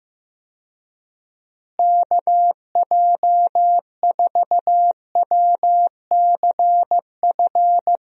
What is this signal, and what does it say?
Morse code, heard as a keyed tone, about 15 wpm, 700 Hz. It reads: KJ4WCF
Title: USA Call Signs - 6 Characters - Repeated 3x - 15wpm